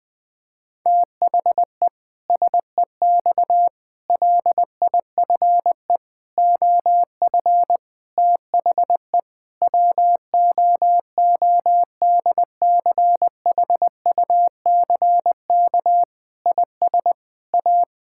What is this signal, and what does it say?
Morse code, heard as a keyed tone, about 20 wpm, 700 Hz. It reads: THE SEX LIFE OF THE WOODCHUCK IS A
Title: Pangram Challenge v2 - 20wpm